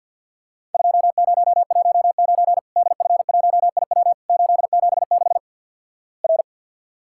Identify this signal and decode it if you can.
Morse code, heard as a keyed tone, about 52 wpm, 700 Hz. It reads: 2019 DF1IW 876 R
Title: Worked All Europe DX Contest - QTC Exchange with Repeats - 50wpm